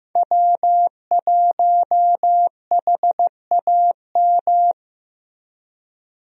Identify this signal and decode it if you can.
Morse code, heard as a keyed tone, about 15 wpm, 700 Hz. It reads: W1HAM
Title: USA Call Signs - 5 Characters - Repeated 2x - 15wpm